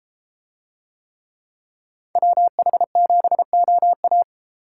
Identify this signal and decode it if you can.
Morse code, heard as a keyed tone, about 32 wpm, 700 Hz. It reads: WH7OA